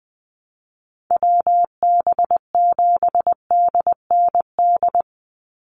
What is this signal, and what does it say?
Morse code, heard as a keyed tone, about 20 wpm, 700 Hz. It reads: WB7DND
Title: Call Signs Repeated 2x - 20wpm